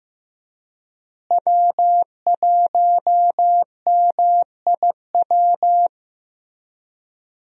Morse code, heard as a keyed tone, about 15 wpm, 700 Hz. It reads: W1MIW
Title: USA Call Signs - 5 Characters - Repeated 3x - 15wpm